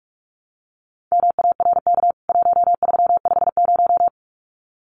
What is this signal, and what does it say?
Morse code, heard as a keyed tone, about 44 wpm, 700 Hz. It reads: MARK 1350
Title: CWT Exchange 45wpm